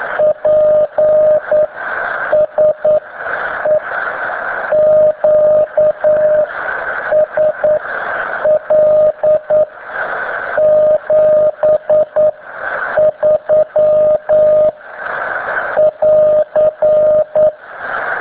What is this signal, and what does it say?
Morse code, heard as a keyed tone, about 9 wpm, spaced as for 7 wpm, 610 Hz, with radio noise. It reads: PSE QSL 73 <AR>